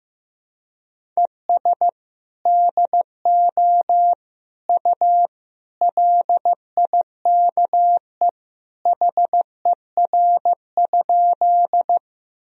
Morse code, heard as a keyed tone, about 15 wpm, 700 Hz. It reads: ES DO U LIKE HER?
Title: Sentences from Top 100 Words and Abbreviations 15wpm